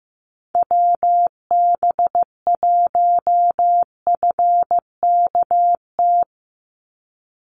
Morse code, heard as a keyed tone, about 15 wpm, 700 Hz. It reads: WB1FKT
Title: USA Call Signs 15wpm